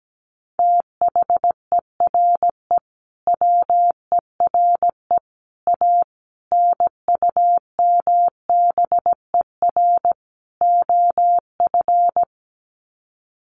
Morse code, heard as a keyed tone, about 17 wpm, 700 Hz. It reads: THERE WERE A NUMBER OF
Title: Sets of 5 Words Encore2 17wpm